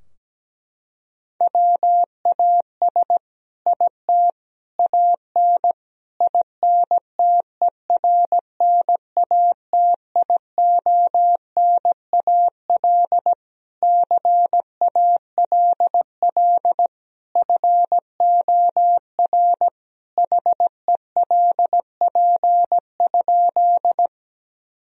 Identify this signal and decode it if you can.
Morse code, heard as a keyed tone, about 17 wpm, 700 Hz. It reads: WAS IT AN INTERNATIONAL CALL FOR HELP?
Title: Sentences from Top 1000 Words - Review Part 4 - 17wpm